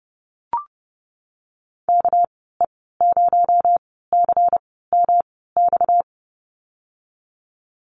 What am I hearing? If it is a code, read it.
Morse code, heard as a keyed tone, about 32 wpm, spaced as for 17 wpm, 700 Hz. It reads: KE0CMX
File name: Lesson 37g - Letter x - Call Signs 30wpm